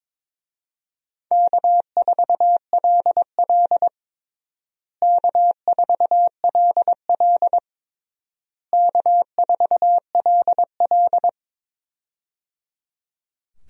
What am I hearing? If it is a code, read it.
Morse code, heard as a keyed tone, about 22 wpm, 700 Hz. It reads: K4LL K4LL K4LL
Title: USA Call Signs - 4 Characters - Repeated 3x - 22wpm